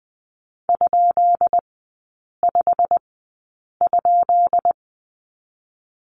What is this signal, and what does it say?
Morse code, heard as a keyed tone, about 20 wpm, spaced as for 8 wpm, 700 Hz. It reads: ?5?
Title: 133 Lesson 22e - Question Mark - Three Characters 20wpm